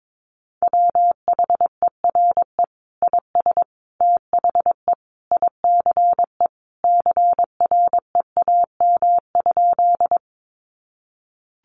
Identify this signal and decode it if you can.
Morse code, heard as a keyed tone, about 22 wpm, 700 Hz. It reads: WHERE IS THE ICE CREAM?
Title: Sentences from Top 1500 Words 22wpm